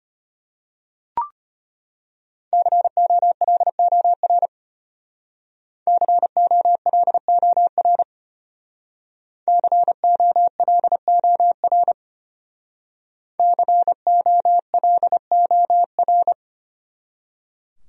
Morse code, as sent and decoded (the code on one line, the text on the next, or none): -.-. --- .-.. --- .-. / -.-. --- .-.. --- .-. / -.-. --- .-.. --- .-. / -.-. --- .-.. --- .-.
COLOR COLOR COLOR COLOR